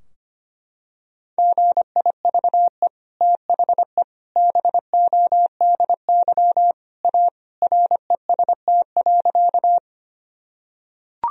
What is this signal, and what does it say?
Morse code, heard as a keyed tone, about 25 wpm, 700 Hz. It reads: GIVE THE BODY A REST.
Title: Sentences from Top 1000 Words - Review Part 2 - 25wpm